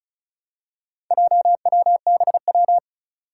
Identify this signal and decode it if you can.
Morse code, heard as a keyed tone, about 36 wpm, 700 Hz. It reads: JWBW
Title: Four Challenging Letter-Number Combinations 35wpm